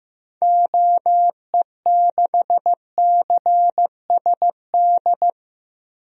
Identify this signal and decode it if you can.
Morse code, heard as a keyed tone, about 15 wpm, 700 Hz. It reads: OE6CSD
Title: DXCC-100 Call Signs - 6 Characters - Repeated 2x - 15wpm